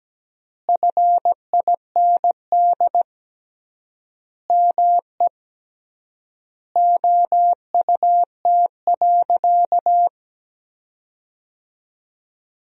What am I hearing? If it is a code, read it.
Morse code, heard as a keyed tone, about 17 wpm, 700 Hz. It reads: FIND ME OUT.
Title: 3x Word-Spacing - Sentences from Top 500 Words - 17wpm